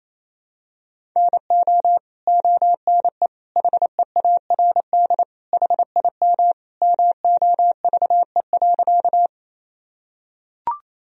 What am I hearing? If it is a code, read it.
Morse code, heard as a keyed tone, about 28 wpm, 700 Hz. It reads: NO ONE HEARD HIM MOVE.
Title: Sentences from Top 1000 Words - Review Part 1 - 28wpm